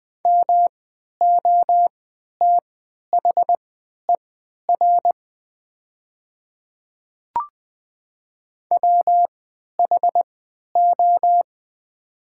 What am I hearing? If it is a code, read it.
Morse code, heard as a keyed tone, about 20 wpm, spaced as for 11 wpm, 700 Hz. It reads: MOTHER WHO
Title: 3x Character Spacing - Top 200 Words - 20wpm